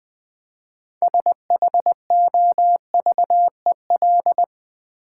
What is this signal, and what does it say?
Morse code, heard as a keyed tone, about 20 wpm, 700 Hz. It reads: SHOVEL
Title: Easy Words with Hints - 20wpm